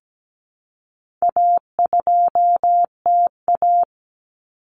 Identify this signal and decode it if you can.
Morse code, heard as a keyed tone, about 17 wpm, 700 Hz. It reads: A2TA
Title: Common Cut Numbers - 4-Digit 17wpm